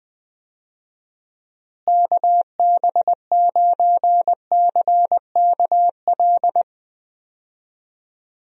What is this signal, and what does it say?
Morse code, heard as a keyed tone, about 20 wpm, 700 Hz. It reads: KB9CKL
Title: USA Call Signs - 6 Characters - Repeated 2x - 20wpm